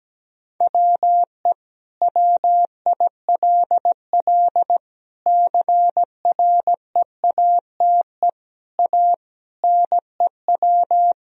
Morse code, heard as a keyed tone, about 17 wpm, 700 Hz. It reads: WE WILL CREATE A NEW
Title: Sentences from Top 1000 Words - Review Part 3 - 17wpm